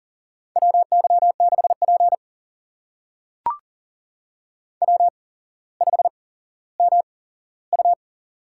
Morse code, heard as a keyed tone, about 40 wpm, spaced as for 17 wpm, 700 Hz. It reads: WY6P W5MU
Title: USA Call Signs - 8x Character Spacing - 40wpm